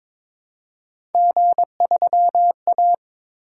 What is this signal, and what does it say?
Morse code, heard as a keyed tone, about 22 wpm, 700 Hz. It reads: G3A
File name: practice / 22wpm / Three Letter-Number Combinations - 22wpm